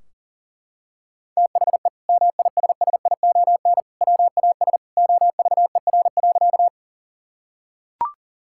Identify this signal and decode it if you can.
Morse code, heard as a keyed tone, about 40 wpm, 700 Hz. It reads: THE MISSION WAS OVER.
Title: Sentences from Top 1600 Words 40wpm